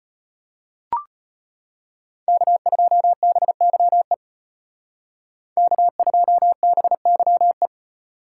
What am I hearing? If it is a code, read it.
Morse code, heard as a keyed tone, about 31 wpm, 700 Hz. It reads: K2BYE K2BYE
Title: Speed Racing - USA 1x3 Call Signs 25wpm